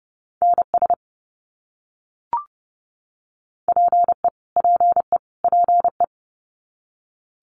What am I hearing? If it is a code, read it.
Morse code, heard as a keyed tone, about 30 wpm, 700 Hz. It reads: NS PE PE PE
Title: Canadian Provinces and Territories - Repeated 3x - 30wpm